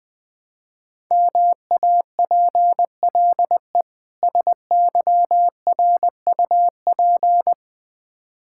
Morse code, heard as a keyed tone, about 20 wpm, 700 Hz. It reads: MAPLE SYRUP
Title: Difficult Words with Hints - 20wpm